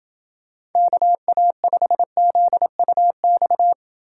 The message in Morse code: KA5ZUX

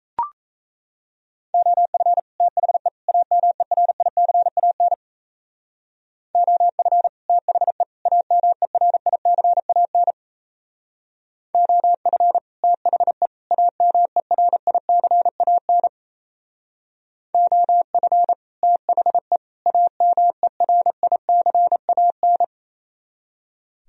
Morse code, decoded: OF THE AMERICAN OF THE AMERICAN OF THE AMERICAN OF THE AMERICAN

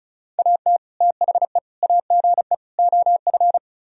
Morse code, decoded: AT THE AGE OF

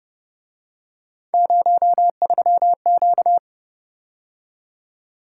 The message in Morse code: 03Q

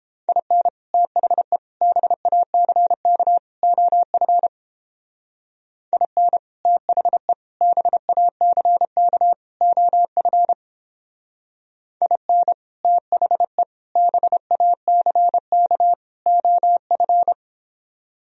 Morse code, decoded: IN THE BACK OF IN THE BACK OF IN THE BACK OF

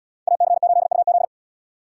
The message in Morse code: AC8UZ

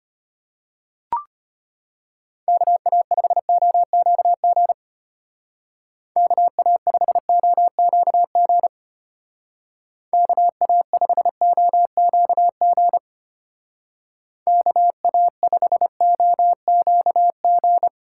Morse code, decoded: KA5OQG KA5OQG KA5OQG KA5OQG

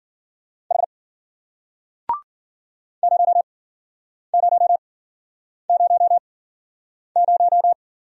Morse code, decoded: H 0000

